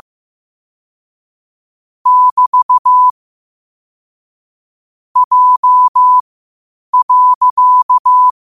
Morse code, heard as a keyed tone, about 15 wpm, spaced as for 8 wpm, 1000 Hz. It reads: <BT> J.
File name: qotdfiles / 8wpm